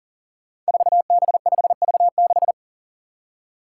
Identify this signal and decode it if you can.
Morse code, heard as a keyed tone, about 38 wpm, 700 Hz. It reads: 4B5V6
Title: Five Challenging Letter-Number Combinations 40wpm